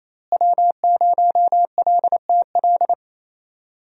Morse code, heard as a keyed tone, about 27 wpm, 700 Hz. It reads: W0LTL
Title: USA Call Signs - 5 Characters - Repeated 3x - 28wpm